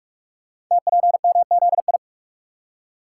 Morse code, heard as a keyed tone, about 44 wpm, 700 Hz. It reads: TPMZI
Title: Code Groups - 5 Letters - 45wpm